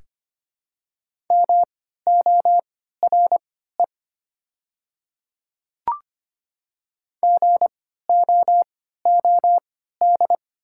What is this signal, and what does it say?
Morse code, heard as a keyed tone, about 25 wpm, spaced as for 14 wpm, 700 Hz. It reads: MORE GOOD
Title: 3x Character Spacing - Top 200 Words - 25wpm